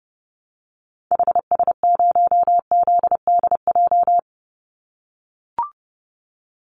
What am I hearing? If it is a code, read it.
Morse code, heard as a keyed tone, about 30 wpm, 700 Hz. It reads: HS0ZDJ
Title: Call Signs - 30wpm